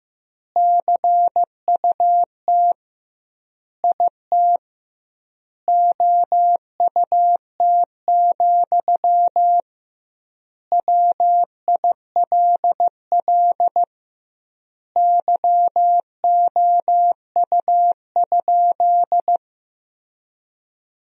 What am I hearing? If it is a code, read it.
Morse code, heard as a keyed tone, about 15 wpm, 700 Hz. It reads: CUT IT OUT, WILL YOU?
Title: 2x Word-Spacing - Sentences from Top 300 Words - 15wpm